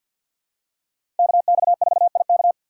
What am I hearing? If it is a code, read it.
Morse code, heard as a keyed tone, about 52 wpm, 700 Hz. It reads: KX4IK
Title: USA Call Signs - 5 Characters - 6x Character Spacing - 50wpm